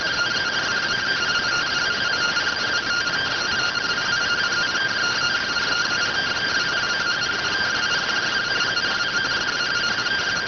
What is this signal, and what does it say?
Two-tone teleprinter signal, about 170 Hz shift.